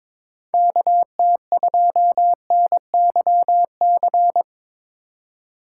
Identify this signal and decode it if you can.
Morse code, heard as a keyed tone, about 22 wpm, 700 Hz. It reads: KT2NYC
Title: USA Call Signs - 6 Characters - 6x Character Spacing - 22wpm